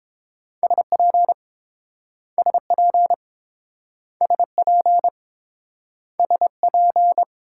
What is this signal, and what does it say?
Morse code, heard as a keyed tone, about 29 wpm, 700 Hz. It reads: SP SP SP SP